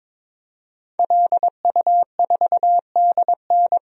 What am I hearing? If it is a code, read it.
Morse code, heard as a keyed tone, about 23 wpm, 700 Hz. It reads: LU4DN
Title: DXCC-100 Call Signs - 5 Characters - Repeated 3x - 22wpm